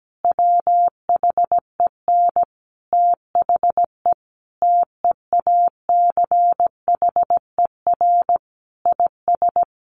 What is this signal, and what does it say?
Morse code, heard as a keyed tone, about 17 wpm, 700 Hz. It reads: WHEN THE TEACHER IS
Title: Additional Easy Phrases 17wpm